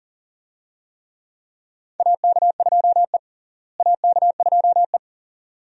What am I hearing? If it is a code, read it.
Morse code, heard as a keyed tone, about 40 wpm, 700 Hz. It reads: AK2E AK2E
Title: USA Call Signs - 4 Characters - Repeated 2x - 40wpm